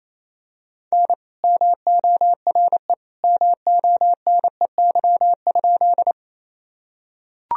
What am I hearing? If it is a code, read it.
Morse code, heard as a keyed tone, about 28 wpm, 700 Hz. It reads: N MORE MONEY?
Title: Sentences from Top 400 Words and Abbreviations 28wpm